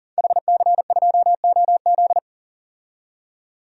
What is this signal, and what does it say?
Morse code, heard as a keyed tone, about 40 wpm, 700 Hz. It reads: HC2OZ
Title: DXCC-100 Call Signs - 5 Characters - 40wpm